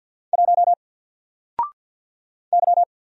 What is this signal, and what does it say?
Morse code, heard as a keyed tone, about 48 wpm, 700 Hz. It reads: 1 Y
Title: Single Character (Letter:Number:Punctuation:Prosign) 50wpm